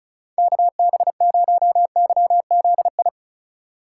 Morse code, heard as a keyed tone, about 43 wpm, 700 Hz. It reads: KB0YZI
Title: Speed Racing - USA 2x3 Call Signs 35wpm